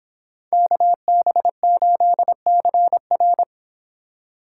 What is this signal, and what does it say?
Morse code, heard as a keyed tone, about 26 wpm, 700 Hz. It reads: KB8CR